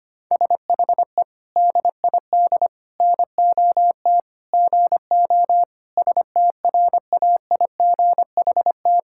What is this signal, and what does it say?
Morse code, heard as a keyed tone, about 25 wpm, 700 Hz. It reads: SHE DID NOT GO STRAIGHT
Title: Sentences from Top 1000 Words - Review Part 6 - 25wpm